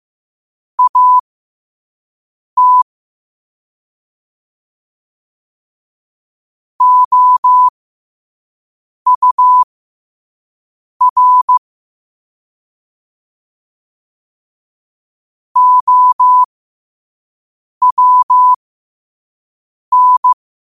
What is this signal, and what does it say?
Morse code, heard as a keyed tone, about 15 wpm, spaced as for 5 wpm, 1000 Hz. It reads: AT OUR OWN